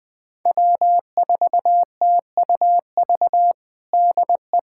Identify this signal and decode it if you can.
Morse code, heard as a keyed tone, about 20 wpm, 700 Hz. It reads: W4TUV DE
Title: USA Callsign DE Callsign - 5 Characters - 20wpm